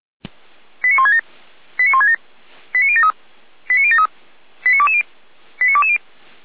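Five-tone selective call (ZVEI), heard as quick runs of five tones, tones of about 70 ms. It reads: 89167 89167 89073 89073 89220 89220
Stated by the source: Five-tone ZVEI selective call received on an SDR by the Feuerwehr Orlamünde fire brigade.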